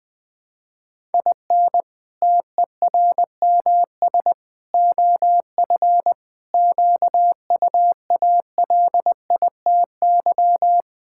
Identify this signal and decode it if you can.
Morse code, heard as a keyed tone, about 20 wpm, 700 Hz. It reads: IN TERMS OF QUALITY
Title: Sentences with 3-Word Phrases - 20wpm